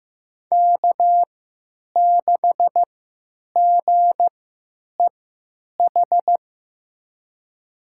Morse code, heard as a keyed tone, about 15 wpm, spaced as for 9 wpm, 700 Hz. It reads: K6GEH
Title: USA Call Signs - 3x Character Spacing - 15wpm